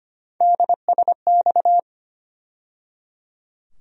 Morse code, heard as a keyed tone, about 25 wpm, 700 Hz. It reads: DSX
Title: USA Call Sign Postfixes - 3 Letters 25wpm